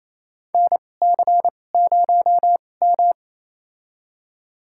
Morse code, heard as a keyed tone, about 27 wpm, spaced as for 20 wpm, 700 Hz. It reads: NC0M
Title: USA Call Signs - 4 Characters - 2x Character Spacing - 28wpm